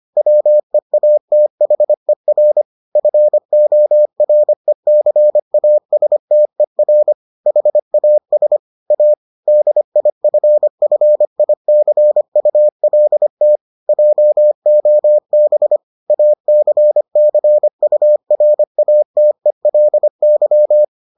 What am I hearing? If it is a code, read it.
Morse code, heard as a keyed tone, about 25 wpm, 600 Hz. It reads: WEATHER FORECASTER HAS A DIFFICULT JOB ACCURATELY